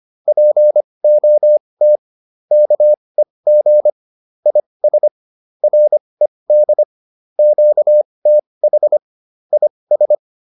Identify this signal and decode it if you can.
Morse code, heard as a keyed tone, about 25 wpm, spaced as for 20 wpm, 600 Hz. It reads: POT KEG IS RED QTH IS